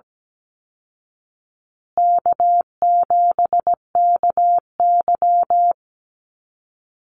Morse code, heard as a keyed tone, about 17 wpm, 700 Hz. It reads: K7KY
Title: Call Signs Repeated 3x - 17wpm